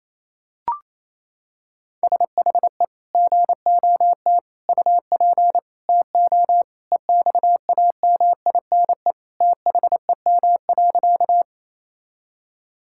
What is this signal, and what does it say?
Morse code, heard as a keyed tone, about 28 wpm, 700 Hz. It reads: SHE GOT UP TO EXAMINE THEM.
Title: Sentences from Top 1600 Words 28wpm